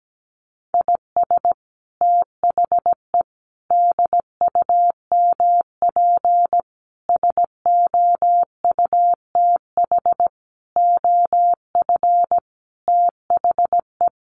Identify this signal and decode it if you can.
Morse code, heard as a keyed tone, about 17 wpm, 700 Hz. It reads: IS THE DUMP SOUTH OF THE
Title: Sentences from Top 1300 Words 17wpm